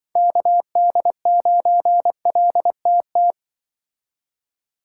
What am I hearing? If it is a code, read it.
Morse code, heard as a keyed tone, about 25 wpm, 700 Hz. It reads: KD9LTT
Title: Speed Racing - USA 2x3 Call Signs 20wpm